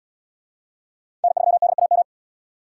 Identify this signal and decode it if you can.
Morse code, heard as a keyed tone, about 77 wpm, 700 Hz. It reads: N3DAM